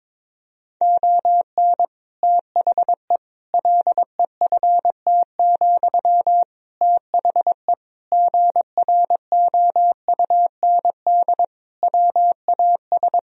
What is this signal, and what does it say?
Morse code, heard as a keyed tone, about 22 wpm, 700 Hz. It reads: ON THE LEFT, THE GROUND WAS
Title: Sentences from Top 1000 Words - Review Part 4 - 22wpm